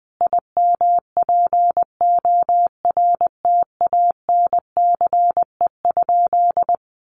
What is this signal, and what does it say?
Morse code, heard as a keyed tone, about 19 wpm, 700 Hz. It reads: IMPORTANCE?
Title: Sentences from Top 1500 Words 20wpm